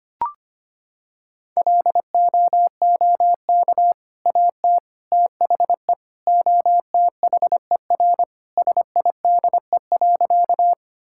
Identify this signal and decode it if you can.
Morse code, heard as a keyed tone, about 25 wpm, 700 Hz. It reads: LOOK AT THE OTHER SIDE.
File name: practice / 25wpm / Sentences with 2-Word Phrases - 25wpm